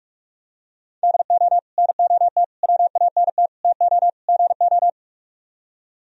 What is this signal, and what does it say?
Morse code, heard as a keyed tone, about 44 wpm, 700 Hz. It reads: DO NOT WANT TO GO